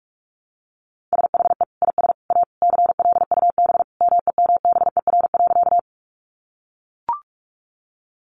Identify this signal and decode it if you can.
Morse code, heard as a keyed tone, about 44 wpm, 700 Hz. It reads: SHE IS A CLUB MEMBER.
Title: Sentences from Top 1200 Words 45wpm